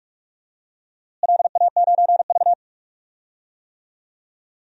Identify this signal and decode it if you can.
Morse code, heard as a keyed tone, about 46 wpm, 700 Hz. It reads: LA9V
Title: DXCC-100 Call Signs - 4 Characters - Repeated 3x - 45wpm